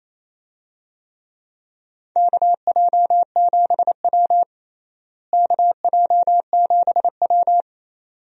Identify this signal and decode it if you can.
Morse code, heard as a keyed tone, about 28 wpm, 700 Hz. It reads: KJ7W KJ7W
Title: USA Call Signs - 4 Characters - Repeated 2x - 28wpm